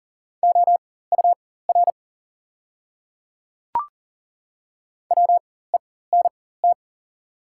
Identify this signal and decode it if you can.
Morse code, heard as a keyed tone, about 40 wpm, spaced as for 19 wpm, 700 Hz. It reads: OUR WENT
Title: 4x Character Spacing - Top 200 Words - 40wpm